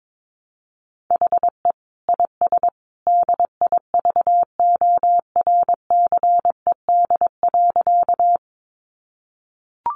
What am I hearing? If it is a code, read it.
Morse code, heard as a keyed tone, about 22 wpm, 700 Hz. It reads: HE IS DIVORCED.
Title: Additional Easy Phrases 22wpm